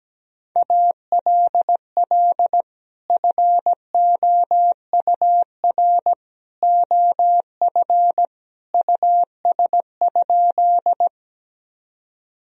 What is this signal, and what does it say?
Morse code, heard as a keyed tone, about 17 wpm, 700 Hz. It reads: ALL FOUR OF US?
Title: Sentences from Top 1000 Words - Review Part 1 - 17wpm